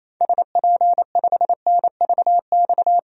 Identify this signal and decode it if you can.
Morse code, heard as a keyed tone, about 28 wpm, 700 Hz. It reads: SP5NVX
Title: DXCC-100 Call Signs - 6 Characters - 28wpm